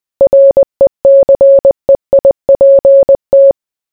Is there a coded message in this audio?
Yes: Morse code from 0.2 to 3.5 s, heard as a keyed tone, about 20 wpm, 550 Hz. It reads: RECEIPT